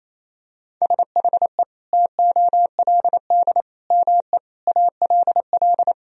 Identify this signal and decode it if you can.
Morse code, heard as a keyed tone, about 28 wpm, 700 Hz. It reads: SHE TOLD ME ALL
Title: Sentences from Top 400 Words and Abbreviations 28wpm